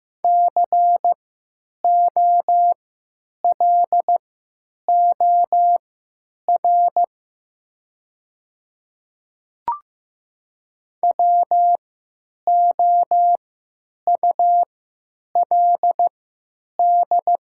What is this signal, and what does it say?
Morse code, heard as a keyed tone, about 15 wpm, spaced as for 9 wpm, 700 Hz. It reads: COLOR WOULD
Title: 3x Character Spacing - Top 500 Words - 15wpm